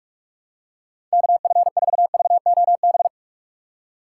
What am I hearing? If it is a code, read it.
Morse code, heard as a keyed tone, about 46 wpm, 700 Hz. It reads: KF4VOB
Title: USA Call Signs - 6 Characters - 6x Character Spacing - 45wpm